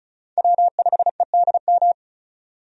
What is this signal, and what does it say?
Morse code, heard as a keyed tone, about 36 wpm, 700 Hz. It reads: W5EDM
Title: USA Call Signs - 5 Characters - 35wpm